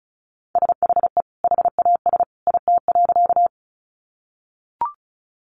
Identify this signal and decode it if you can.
Morse code, heard as a keyed tone, about 34 wpm, 700 Hz. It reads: SHE HAS IT.
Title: Sentences from Top 100 Words 35wpm